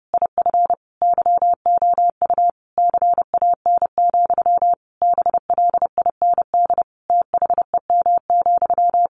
Morse code, heard as a keyed tone, about 30 wpm, 700 Hz. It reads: IF YOU CAN, BLIND THEM,
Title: A Sound Of Thunder 30wpm